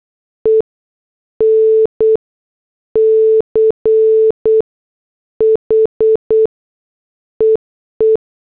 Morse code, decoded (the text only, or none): ENCH EE